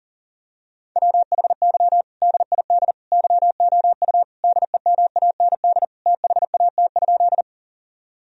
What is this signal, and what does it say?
Morse code, heard as a keyed tone, about 40 wpm, 700 Hz. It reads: WHY DID YOU DEMAND THAT?
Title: Sentences from Top 1000 Words - Review Part 4 - 40wpm